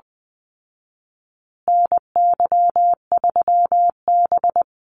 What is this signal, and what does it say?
Morse code, heard as a keyed tone, about 20 wpm, 700 Hz. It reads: NY3B
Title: Call Signs Repeated 3x - 20wpm